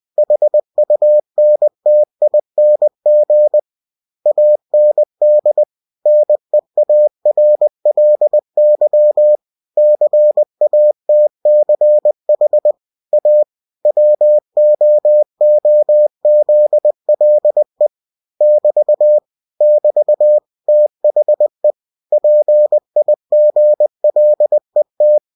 Morse code, heard as a keyed tone, about 20 wpm, 600 Hz. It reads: HUNTING AND NEARLY CATCH A WOOZLE <BT> <BT> THE PIGLET
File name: cw-books / winnie-the-pooh / winnie-20wpm-page-0020